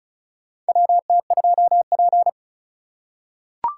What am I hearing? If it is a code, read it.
Morse code, heard as a keyed tone, about 34 wpm, 700 Hz. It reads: WT2P
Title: USA Call Signs - 4 Characters - 2x Character Spacing - 35wpm